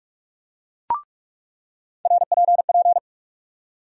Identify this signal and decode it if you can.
Morse code, heard as a keyed tone, about 43 wpm, 700 Hz. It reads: RPP
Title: USA Call Sign Postfixes - 3 Letters 45wpm